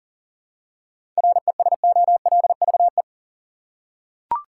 Morse code, heard as a keyed tone, about 40 wpm, 700 Hz. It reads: RESOLVE
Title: Seven Letter Words - 40wpm